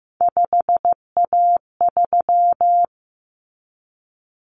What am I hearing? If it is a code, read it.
Morse code, heard as a keyed tone, about 15 wpm, 700 Hz. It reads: 5A3
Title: Common Cut Numbers - 3-Digit 15wpm